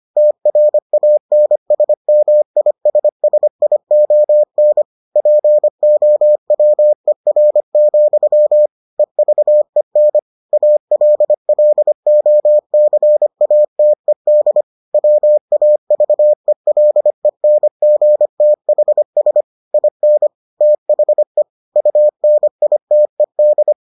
Morse code, heard as a keyed tone, about 25 wpm, 600 Hz. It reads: TRANSMISSION POWER, EVEN ALLOCATED WAVELENGTHS IN THE UNITED